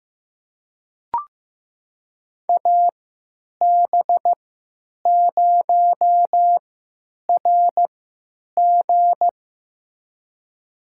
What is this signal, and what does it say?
Morse code, heard as a keyed tone, about 15 wpm, spaced as for 9 wpm, 700 Hz. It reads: AB0RG
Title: USA Call Signs - 5 Characters - 3x Character Spacing - 15wpm